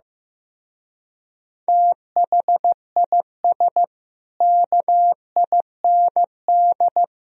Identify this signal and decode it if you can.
Morse code, heard as a keyed tone, about 15 wpm, 700 Hz. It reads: THIS KIND
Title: Speed-Racing - Sets of 3 Words 15wpm